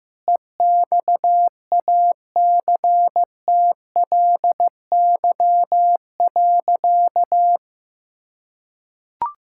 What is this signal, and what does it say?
Morse code, heard as a keyed tone, about 15 wpm, 700 Hz. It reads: EXACTLY.
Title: Sentences from Top 1000 Words - Review Part 6 - 15wpm